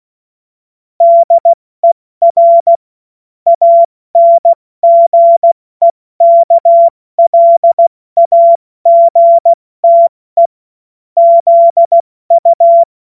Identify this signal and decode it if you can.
Morse code, heard as a keyed tone, about 16 wpm, 680 Hz. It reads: DER ANGEKLAGTE ZU
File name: Morsekurs / Witze / Witz 29. August 2023 70 F80